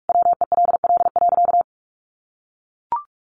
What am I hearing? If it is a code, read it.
Morse code, heard as a keyed tone, about 44 wpm, 700 Hz. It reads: WELL.
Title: Sentences from Top 1200 Words 45wpm